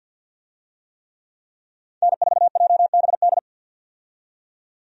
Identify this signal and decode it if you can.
Morse code, heard as a keyed tone, about 52 wpm, 700 Hz. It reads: N4JBD
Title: USA Call Signs - 5 Characters - 50wpm